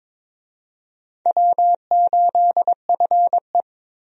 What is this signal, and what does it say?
Morse code, heard as a keyed tone, about 22 wpm, 700 Hz. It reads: W8FE
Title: USA Call Signs - 4 Characters - 22wpm